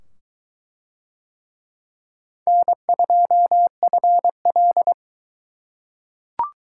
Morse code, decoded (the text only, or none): N2FL